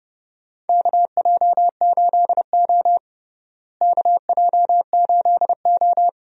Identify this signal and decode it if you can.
Morse code, heard as a keyed tone, about 30 wpm, 700 Hz. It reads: KJ8O KJ8O
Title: USA Call Signs - 4 Characters - Repeated 2x - 30wpm